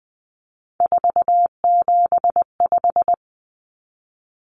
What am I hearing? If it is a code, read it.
Morse code, heard as a keyed tone, about 20 wpm, 700 Hz. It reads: 475
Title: Common Cut Numbers - 3-Digit 20wpm